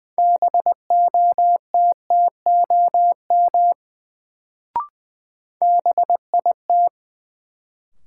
Morse code, heard as a keyed tone, about 20 wpm, 700 Hz. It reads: BOTTOM BIT
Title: Sentences from Top 900 Words - New Word Drill - 20wpm